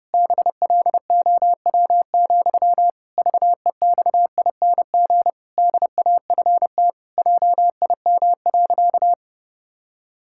Morse code, heard as a keyed tone, about 31 wpm, 700 Hz. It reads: BLOW, VEXING DAFT JIM.
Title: Pangram Challenge v2 - 30wpm